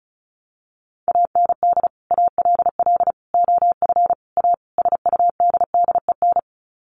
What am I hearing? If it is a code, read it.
Morse code, heard as a keyed tone, about 34 wpm, 700 Hz. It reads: AND ALL OF A SUDDEN
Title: Sets of 5 Words 35wpm